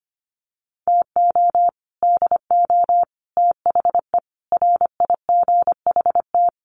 Morse code, heard as a keyed tone, about 25 wpm, 700 Hz. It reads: TO DO THE RIGHT
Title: Sets of 5 Words 25wpm